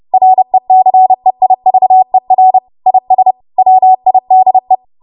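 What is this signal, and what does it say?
Morse code, heard as a keyed tone, about 30 wpm, 750 Hz. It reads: RECEIVER IS WIDE
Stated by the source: Morse practice transmission at 30 WPM.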